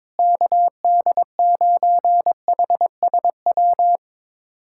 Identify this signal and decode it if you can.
Morse code, heard as a keyed tone, about 22 wpm, 700 Hz. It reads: KD9HSW
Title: USA Callsign DE Callsign - 6 Characters - 22wpm